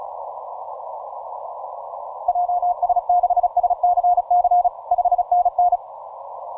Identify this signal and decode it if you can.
Morse code, heard as a keyed tone, about 34 wpm, 710 Hz. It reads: JS6SCC 5NN